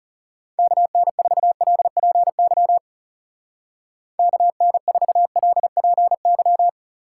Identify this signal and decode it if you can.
Morse code, heard as a keyed tone, about 34 wpm, 700 Hz. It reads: KN4LPY KN4LPY
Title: Speed Racing - USA Call Signs 30wpm